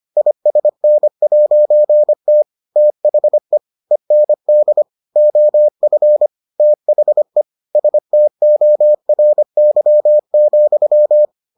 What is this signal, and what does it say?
Morse code, heard as a keyed tone, about 25 wpm, 600 Hz. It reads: ISN'T THE END OF THE STORY,